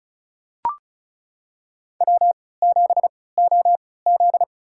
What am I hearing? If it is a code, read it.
Morse code, heard as a keyed tone, about 36 wpm, spaced as for 20 wpm, 700 Hz. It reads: W7OZ
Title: USA Call Signs - 4 Characters - 3x Character Spacing - 35wpm